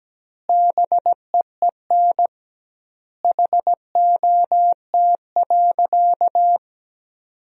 Morse code, heard as a keyed tone, about 17 wpm, 700 Hz. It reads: BEEN HOT.
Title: 2x Word-Spacing - Sentences from Top 500 Words - 17wpm